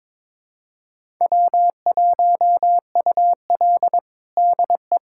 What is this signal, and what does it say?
Morse code, heard as a keyed tone, about 22 wpm, 700 Hz. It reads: W1UL DE
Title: USA Callsign DE Callsign - 4 Characters - 22wpm